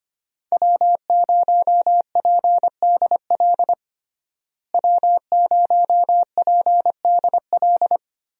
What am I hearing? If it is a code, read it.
Morse code, heard as a keyed tone, about 25 wpm, 700 Hz. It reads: W0PDL W0PDL
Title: USA Call Signs - 5 Characters - Repeated 3x - 25wpm